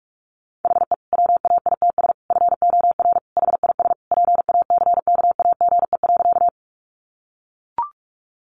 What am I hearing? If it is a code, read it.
Morse code, heard as a keyed tone, about 44 wpm, 700 Hz. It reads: HE WAITS FOR HIS PACKAGE.
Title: Additional Easy Phrases 45wpm